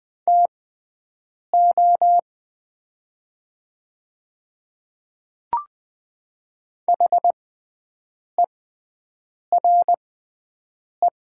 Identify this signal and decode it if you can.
Morse code, heard as a keyed tone, about 20 wpm, spaced as for 7 wpm, 700 Hz. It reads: TO HERE
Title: 6x Character Spacing - Top 200 Words - 20wpm